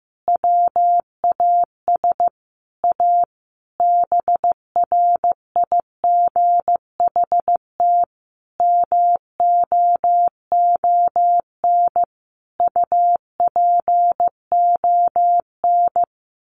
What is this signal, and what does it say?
Morse code, heard as a keyed tone, about 15 wpm, 700 Hz. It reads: WAS A BRIGHT MOON UPON
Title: Sentences from Top 1500 Words 15wpm